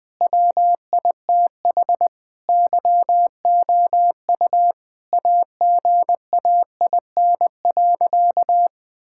Morse code, heard as a keyed tone, about 20 wpm, 700 Hz. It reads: WITH YOU AGAIN.
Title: Sentences from Top 1600 Words 20wpm